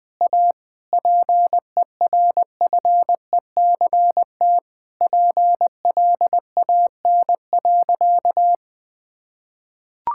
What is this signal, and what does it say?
Morse code, heard as a keyed tone, about 19 wpm, 700 Hz. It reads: A PERFECT PLAN.
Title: Sentences with 7-Word Phrases - Spoken Beginning - 20wpm